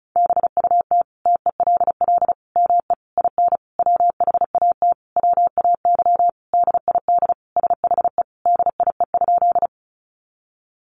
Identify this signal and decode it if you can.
Morse code, heard as a keyed tone, about 34 wpm, 700 Hz. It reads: BUT TELL ME IN WHAT WAY DID SHE DIE?
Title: Sentences from Top 1300 Words 35wpm